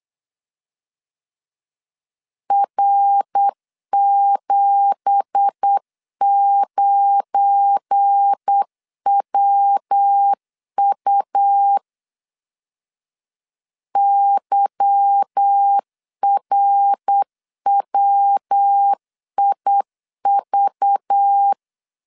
Morse code, heard as a keyed tone, about 8 wpm, 800 Hz. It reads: R79WU YRWIV
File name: CW / DET / 1 9 1500C S C 8wpm 5653